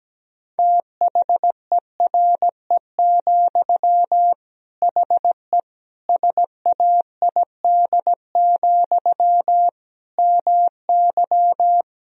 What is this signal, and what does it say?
Morse code, heard as a keyed tone, about 17 wpm, 700 Hz. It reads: THERE, HE SAID, MY
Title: Sentences from Top 1000 Words - Review Part 2 - 17wpm